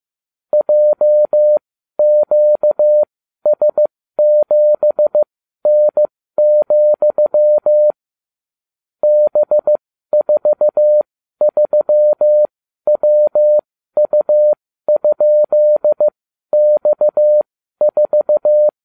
Morse code, heard as a keyed tone, about 15 wpm, spaced as for 12 wpm, 600 Hz. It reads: JQS7N, B43WU?X4